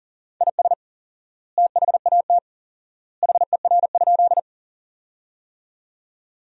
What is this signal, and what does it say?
Morse code, heard as a keyed tone, about 40 wpm, 700 Hz. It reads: IS THAT HER?
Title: 4x Word-Spacing - Sentences from Top 100 Words - 40wpm